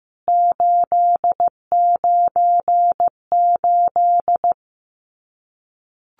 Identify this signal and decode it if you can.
Morse code, heard as a keyed tone, about 15 wpm, 700 Hz. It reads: 898
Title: CWT Exchange 15wpm